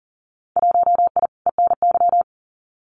Morse code, heard as a keyed tone, about 40 wpm, 700 Hz. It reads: 1I ENY
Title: Winter Field Day Exchanges Encore 40wpm